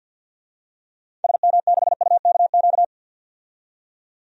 Morse code, heard as a keyed tone, about 52 wpm, 700 Hz. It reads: SM6UKQ
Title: DXCC-100 Call Signs - 6 Characters - 50wpm